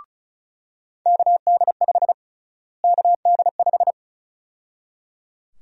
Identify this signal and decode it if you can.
Morse code, heard as a keyed tone, about 36 wpm, 700 Hz. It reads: KD5 KD5
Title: USA Call Sign Prefixes - Repeated 2x - 35wpm